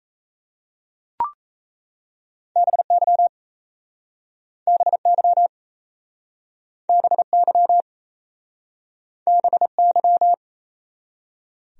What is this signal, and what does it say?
Morse code, heard as a keyed tone, about 38 wpm, 700 Hz. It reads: BY BY BY BY